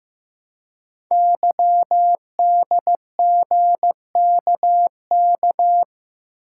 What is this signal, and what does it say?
Morse code, heard as a keyed tone, about 15 wpm, 700 Hz. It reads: YDGKK
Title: Code Groups - 5 Letters - 15wpm